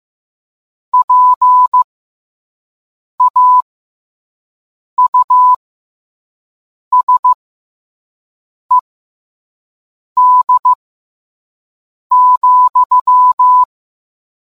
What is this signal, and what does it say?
Morse code, heard as a keyed tone, about 15 wpm, spaced as for 5 wpm, 1000 Hz. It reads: PAUSED,